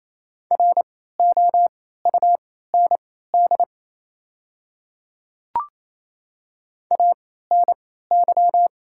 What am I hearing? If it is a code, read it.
Morse code, heard as a keyed tone, about 28 wpm, spaced as for 16 wpm, 700 Hz. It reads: ROUND ANY
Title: 3x Character Spacing - Top 300 Words - 28wpm